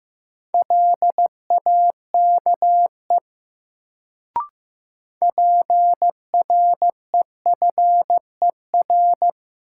Morse code, heard as a keyed tone, about 15 wpm, 700 Hz. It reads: LAKE PREFER
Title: Sentences from Top 1100 Words - New Word Drill Alt - 15wpm